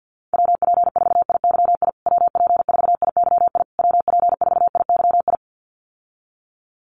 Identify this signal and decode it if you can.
Morse code, heard as a keyed tone, about 50 wpm, 700 Hz. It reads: WP4IYI WP4IYI WP4IYI
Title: USA Call Signs Repeated 3x 50wpm